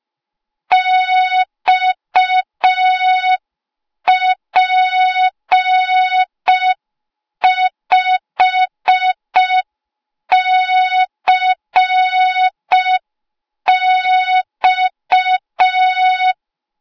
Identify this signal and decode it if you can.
Morse code, heard as a keyed tone, about 5 wpm, 3700 Hz. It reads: XP5CX
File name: Morze-HA2MN / hang / 08KXC HA2MN